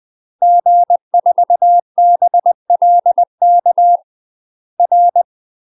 Morse code, heard as a keyed tone, about 20 wpm, 700 Hz. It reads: G4BLK R
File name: UK MASTER QSOs TRAINING / 20WPM / UK MASTER QSOs 0022